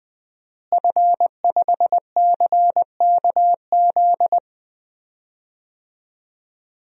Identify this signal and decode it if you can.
Morse code, heard as a keyed tone, about 20 wpm, 700 Hz. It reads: F5CKZ
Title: DXCC-100 Call Signs - 5 Characters - Repeated 2x - 20wpm